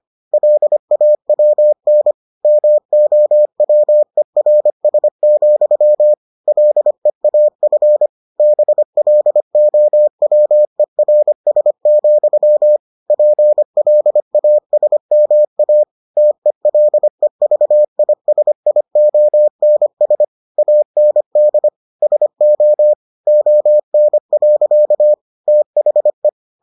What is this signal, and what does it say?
Morse code, heard as a keyed tone, about 25 wpm, 600 Hz. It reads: LAWN MOWERS, LEAF BLOWERS, PLASMA TELEVISIONS AND SO ON. THE